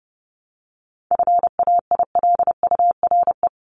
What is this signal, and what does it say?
Morse code, heard as a keyed tone, about 30 wpm, 700 Hz. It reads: FAILURE
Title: Top 751-1000 Words 30wpm